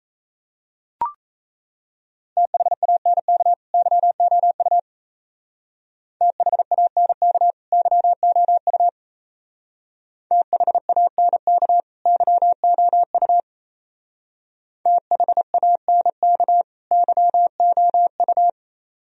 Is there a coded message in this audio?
Yes: Morse code from 2.4 to 18.5 s, heard as a keyed tone, about 36 wpm, 700 Hz. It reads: THANK YOU THANK YOU THANK YOU THANK YOU